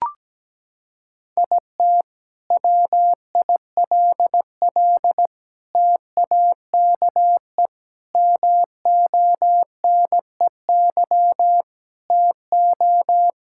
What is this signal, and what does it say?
Morse code, heard as a keyed tone, about 17 wpm, 700 Hz. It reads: IT WILL TAKE MONEY TO
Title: Sentences from Top 1600 Words 17wpm